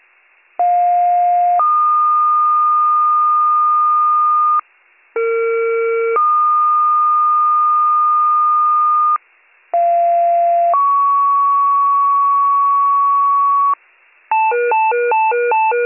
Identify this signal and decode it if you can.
Two-tone page, heard as a one-second tone, then a three-second tone, about 710 then 1220 Hz, about 460 then 1220 Hz, about 690 then 1120 Hz. It is heard from 0.6 to 13.7 s.